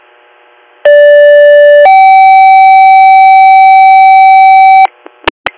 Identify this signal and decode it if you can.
Two-tone page, heard as a one-second tone, then a three-second tone, about 580 then 770 Hz. It is heard from 0.9 to 4.9 s.